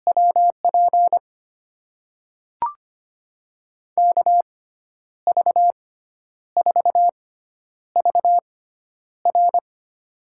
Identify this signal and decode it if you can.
Morse code, heard as a keyed tone, about 25 wpm, spaced as for 10 wpm, 700 Hz. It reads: WP KV4VR
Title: USA Call Signs - 5 Characters - 6x Character Spacing - 25wpm